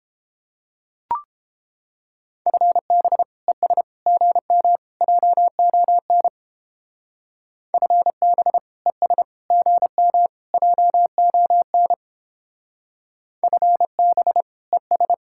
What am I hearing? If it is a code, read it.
Morse code, heard as a keyed tone, about 29 wpm, 700 Hz. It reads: FB ES GM JON FB ES GM JON FB ES